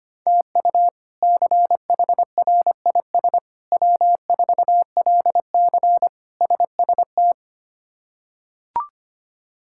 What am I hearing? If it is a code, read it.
Morse code, heard as a keyed tone, about 25 wpm, 700 Hz. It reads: TU CHRIS W4LC SST
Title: K1USN SST QSO Practice - 25wpm